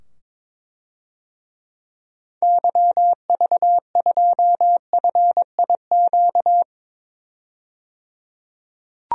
Morse code, heard as a keyed tone, about 22 wpm, 700 Hz. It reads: YV2FIQ
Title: DXCC-100 Call Signs - 6 Characters - 22wpm